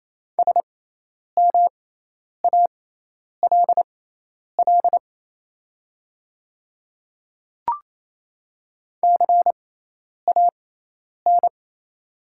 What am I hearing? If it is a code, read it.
Morse code, heard as a keyed tone, about 28 wpm, spaced as for 10 wpm, 690 Hz. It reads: SMALL CAN
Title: 6x Character Spacing - Top 200 Words - 28wpm